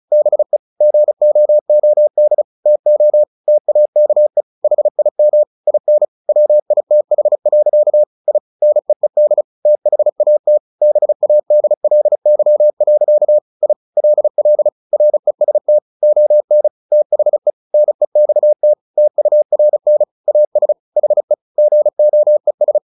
Morse code, heard as a keyed tone, about 34 wpm, 600 Hz. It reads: BE GOOD TO TAKE HIM IN WITH. I NEED THAT BADLY. I LL REST ON THE NEXT TURN AS HE GOES
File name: the old man and the sea 2 35wpm